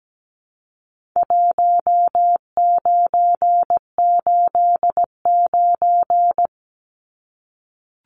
Morse code, heard as a keyed tone, about 17 wpm, 700 Hz. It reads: 1989
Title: Numbers - 4-Digit 17wpm